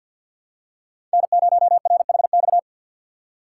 Morse code, heard as a keyed tone, about 52 wpm, 690 Hz. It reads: N0RHX
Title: USA Call Signs - 5 Characters - 2x Character Spacing - 50wpm